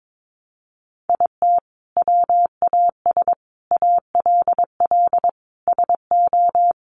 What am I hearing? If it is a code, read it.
Morse code, heard as a keyed tone, about 22 wpm, 700 Hz. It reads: IT WAS ALL SO
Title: Sentences from Top 1000 Words 22wpm